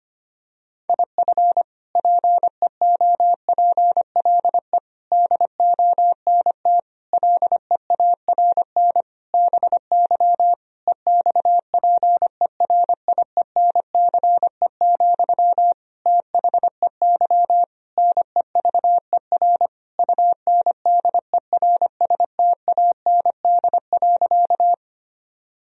Morse code, heard as a keyed tone, about 25 wpm, 700 Hz. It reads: IF PEOPLE DONT LEARN BY EXPERIENCE, THEY NEVER UNDERSTAND.
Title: Sentences from Top 1000 Words - Review Part 3 - 25wpm